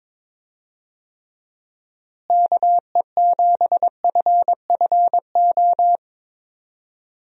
Morse code, heard as a keyed tone, about 21 wpm, 700 Hz. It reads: KE7FFO
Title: USA Call Signs - 6 Characters - Repeated 3x - 22wpm